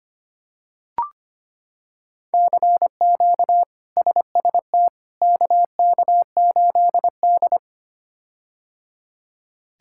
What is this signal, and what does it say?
Morse code, heard as a keyed tone, about 25 wpm, 700 Hz. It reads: CQ SST KK8D
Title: K1USN SST QSO Practice - 25wpm